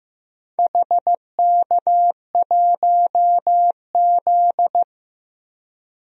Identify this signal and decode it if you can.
Morse code, heard as a keyed tone, about 15 wpm, 700 Hz. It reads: HK1Z
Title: DXCC-100 Call Signs - 4 Characters - Repeated 3x - 15wpm